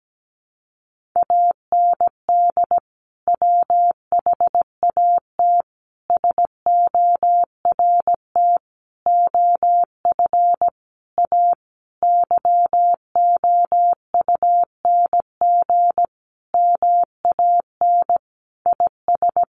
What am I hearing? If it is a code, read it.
Morse code, heard as a keyed tone, about 17 wpm, 700 Hz. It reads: AND WHAT SORT OF A YOUNG MAN IS
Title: Sentences from Top 1300 Words 17wpm